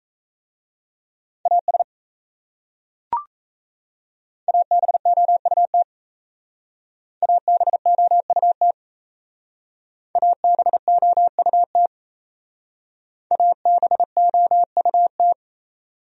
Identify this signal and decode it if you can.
Morse code, heard as a keyed tone, about 36 wpm, 700 Hz. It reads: AS ABOUT ABOUT ABOUT ABOUT